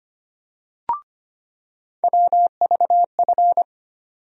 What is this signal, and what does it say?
Morse code, heard as a keyed tone, about 25 wpm, 700 Hz. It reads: WVF
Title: USA Call Sign Postfixes - 3 Letters 25wpm